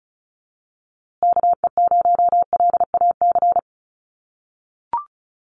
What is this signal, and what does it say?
Morse code, heard as a keyed tone, about 34 wpm, 700 Hz. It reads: KE0LAC
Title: USA Call Signs Repeated 3x 35wpm